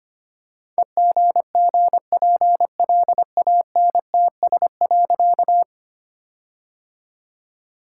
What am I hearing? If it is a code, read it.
Morse code, heard as a keyed tone, about 25 wpm, 700 Hz. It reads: EGGPLANTS.
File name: practice / 25wpm / All-Age Appropriate Jokes - 25wpm